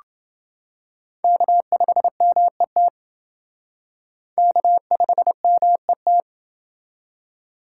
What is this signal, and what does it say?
Morse code, heard as a keyed tone, about 29 wpm, 700 Hz. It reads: K5MET K5MET